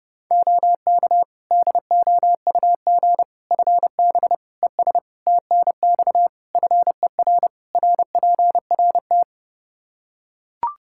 Morse code, heard as a keyed tone, about 32 wpm, 700 Hz. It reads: OK DOUG FB ES TNX FER RPRT
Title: Practice QSO - Exchange 2 - Info - 30wpm